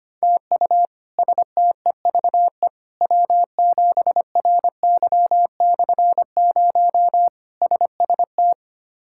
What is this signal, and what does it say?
Morse code, heard as a keyed tone, about 25 wpm, 700 Hz. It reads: TU STEVE W7RY/0 SST
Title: K1USN SST QSO Practice - 25wpm